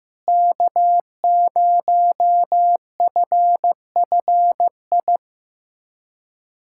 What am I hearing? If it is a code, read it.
Morse code, heard as a keyed tone, about 15 wpm, 700 Hz. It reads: K0FFI
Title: USA Callsign DE Callsign 15wpm